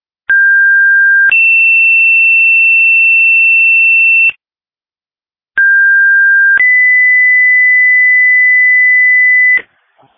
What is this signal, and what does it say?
Two-tone page, heard as a one-second tone, then a three-second tone, about 1600 then 2700 Hz, about 1600 then 2000 Hz, with noise. It is heard from 0.3 to 9.6 s.
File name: dispatch / Sumner Fire 2026 08 20 19 07 44